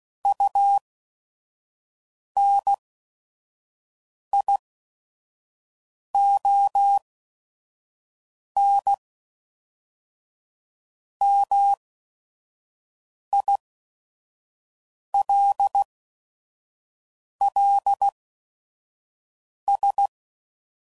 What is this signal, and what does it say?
Morse code, heard as a keyed tone, about 16 wpm, spaced as for 5 wpm, 800 Hz. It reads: UNIONMILLS
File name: CW Code Groups @5WPM / fromSM's / Cities